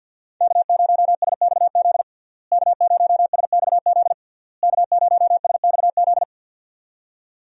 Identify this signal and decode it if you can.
Morse code, heard as a keyed tone, about 50 wpm, 700 Hz. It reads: K0SXZ K0SXZ K0SXZ
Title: USA Call Signs - 5 Characters - Repeated 3x - 50wpm